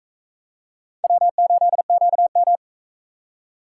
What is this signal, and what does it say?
Morse code, heard as a keyed tone, about 43 wpm, 700 Hz. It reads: W8QM